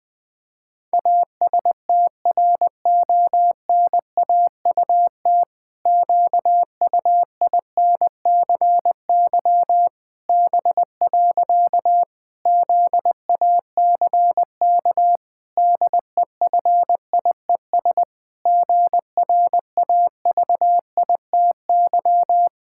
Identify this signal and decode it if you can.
Morse code, heard as a keyed tone, about 21 wpm, 700 Hz. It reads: ASTRONAUT QUINCY B. ZACK DEFIES GRAVITY
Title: Pangram Challenge v2 - 20wpm